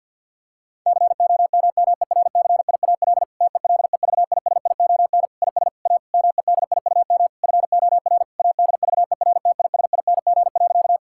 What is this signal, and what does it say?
Morse code, heard as a keyed tone, about 48 wpm, 700 Hz. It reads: COMMERCIAL TELEVISION IS A MEDIUM FOR ADVERTISING.
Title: Sentences from Top 1000 Words - Review Part 6 - 50wpm